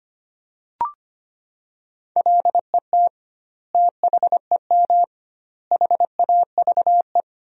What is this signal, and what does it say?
Morse code, heard as a keyed tone, about 25 wpm, 700 Hz. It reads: LET THEM HAVE
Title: 2x Word-Spacing - Sentences from Top 300 Words - 25wpm